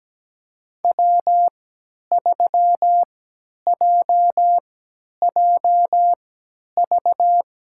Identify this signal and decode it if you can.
Morse code, heard as a keyed tone, about 17 wpm, spaced as for 10 wpm, 700 Hz. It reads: W3JJV
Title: USA Call Signs - 5 Characters - 3x Character Spacing - 17wpm